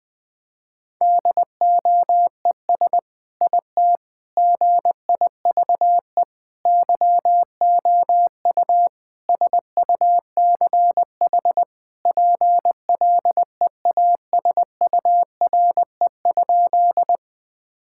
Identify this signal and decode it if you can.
Morse code, heard as a keyed tone, about 21 wpm, 700 Hz. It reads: DOES IT GIVE YOU SUCH PLEASURE?
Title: Sentences from Top 1600 Words 20wpm